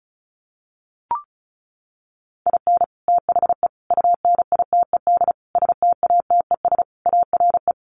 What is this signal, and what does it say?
Morse code, heard as a keyed tone, about 34 wpm, 700 Hz. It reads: IN THE UNITED STATES ARE
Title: Sets of 5 Words Encore 35wpm